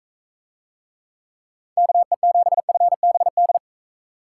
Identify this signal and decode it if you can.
Morse code, heard as a keyed tone, about 41 wpm, 700 Hz. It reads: KE7FBD